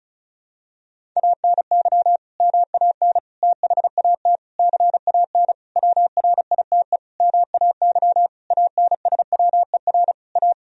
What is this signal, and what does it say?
Morse code, heard as a keyed tone, about 36 wpm, 700 Hz. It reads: ANY MAN THAT CAN WRITE MAY ANSWER A
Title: Sentences from Top 1000 Words - Review Part 1 - 35wpm